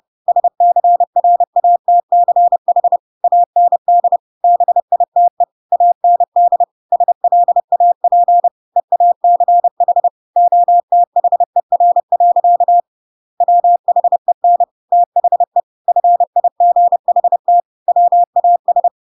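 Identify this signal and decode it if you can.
Morse code, heard as a keyed tone, about 30 wpm, 700 Hz. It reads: SCRATCH AND BITE AND SLAP EACH OTHER. WHEN THE FIGHT WAS
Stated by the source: Chapter 2 of The Adventures of Pinocchio read as a book in Morse code (CW) at 30 wpm, by ik1hge.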